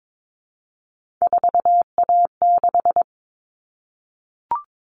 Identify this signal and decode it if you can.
Morse code, heard as a keyed tone, about 22 wpm, 700 Hz. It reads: 4A6
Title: All Cut Numbers - 3-Digit 22wpm